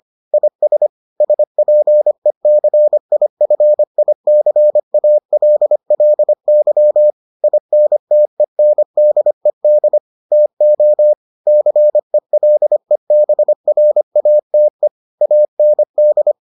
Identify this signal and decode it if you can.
Morse code, heard as a keyed tone, about 25 wpm, 600 Hz. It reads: IS SPECIFICALLY INTENDED TO CELEBRATE AND